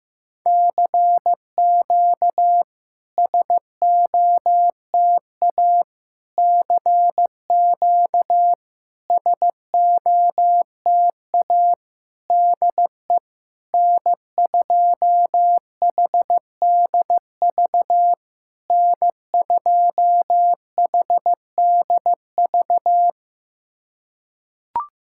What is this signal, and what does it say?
Morse code, heard as a keyed tone, about 15 wpm, 700 Hz. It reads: CQ SOTA CQ SOTA DE N2HDV N2HDV
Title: SOTA QSO Practice 15wpm